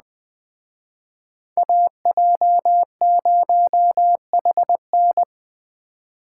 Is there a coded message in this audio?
Yes: Morse code from 1.6 to 5.2 s, heard as a keyed tone, about 20 wpm, 700 Hz. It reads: AJ0HN